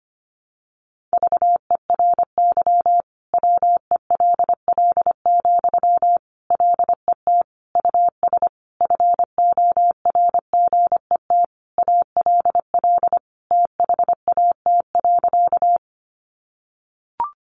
Morse code, heard as a keyed tone, about 25 wpm, 700 Hz. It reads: VERY WELL, LET US FORGET ALL THAT.
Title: Sentences from Top 1200 Words 25wpm